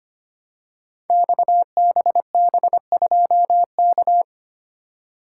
Morse code, heard as a keyed tone, about 25 wpm, 700 Hz. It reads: XBB2K
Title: Code Groups - 5 Letter:Numbers - 25wpm